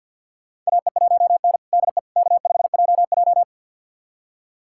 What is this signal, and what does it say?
Morse code, heard as a keyed tone, about 52 wpm, 700 Hz. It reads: AE1N DE K5JJ
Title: USA Callsign DE Callsign - 4 Characters - 50wpm